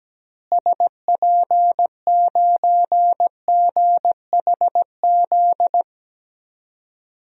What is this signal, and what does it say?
Morse code, heard as a keyed tone, about 17 wpm, 700 Hz. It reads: SP9GHZ
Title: DXCC-100 Call Signs - 6 Characters - Repeated 2x - 17wpm